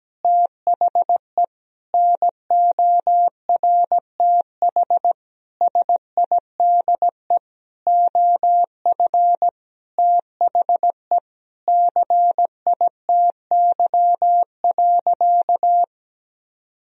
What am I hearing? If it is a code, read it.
Morse code, heard as a keyed tone, about 17 wpm, 700 Hz. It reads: THE NORTH SIDE OF THE CITY.
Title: Sentences from Top 1000 Words - Review Part 1 - 17wpm